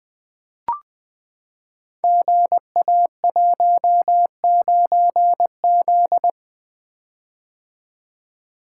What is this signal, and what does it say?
Morse code, heard as a keyed tone, about 20 wpm, 700 Hz. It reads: GA19Z
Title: Code Groups - 5 Letter:Numbers - 20wpm